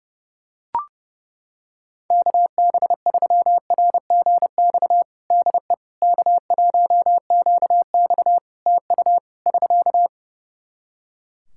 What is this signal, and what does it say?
Morse code, heard as a keyed tone, about 30 wpm, 700 Hz. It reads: KB3RGX DE K1QX TU <SK>
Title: Practice QSO - Exchange 4 - Ending - 30wpm